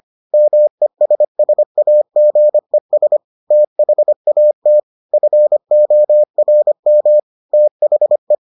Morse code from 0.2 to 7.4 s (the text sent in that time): MESSAGES THAT FORM